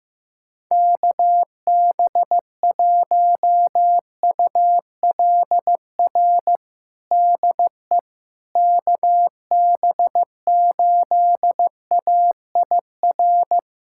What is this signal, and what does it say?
Morse code, heard as a keyed tone, about 15 wpm, 700 Hz. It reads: KB1ULR DE KB8AIR
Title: USA Callsign DE Callsign 15wpm